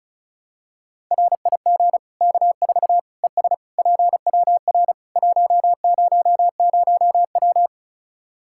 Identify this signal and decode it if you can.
Morse code, heard as a keyed tone, about 36 wpm, 700 Hz. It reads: RIG K4 ES PWR 100W
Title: K1USN SST QSO Practice 35wpm